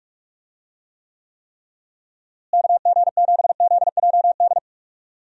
Morse code, heard as a keyed tone, about 42 wpm, 700 Hz. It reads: KG7ZJD
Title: USA Call Signs - 6 Characters - Repeated 3x - 45wpm